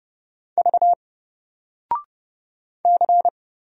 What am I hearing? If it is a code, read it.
Morse code, heard as a keyed tone, about 30 wpm, 700 Hz. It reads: V C
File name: Single-Letter-Number-30wpm